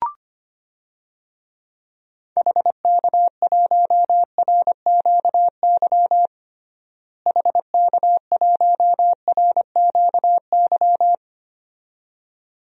Morse code, heard as a keyed tone, about 25 wpm, 700 Hz. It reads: HK1RQY HK1RQY
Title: DXCC-100 Call Signs - 6 Characters - Repeated 2x - 25wpm